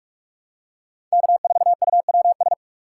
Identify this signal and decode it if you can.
Morse code, heard as a keyed tone, about 41 wpm, 700 Hz. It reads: K4UWS
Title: USA Call Signs - 5 Characters - 3x Character Spacing - 45wpm